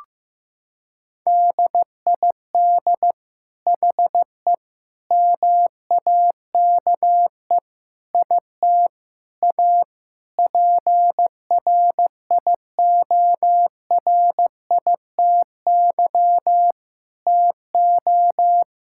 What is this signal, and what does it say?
Morse code, heard as a keyed tone, about 15 wpm, 700 Hz. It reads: DID HE MAKE IT A PRIORITY TO
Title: Sentences from Top 1600 Words 15wpm